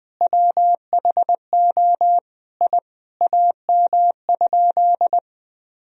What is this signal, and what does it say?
Morse code, heard as a keyed tone, about 20 wpm, 700 Hz. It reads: WHO I AM?
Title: Sentences from Top 1000 Words - Review Part 2 - 20wpm